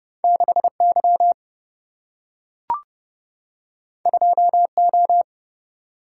Morse code, heard as a keyed tone, about 31 wpm, 700 Hz. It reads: 6 Y 2 O
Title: Single-Letter Single-Number 30wpm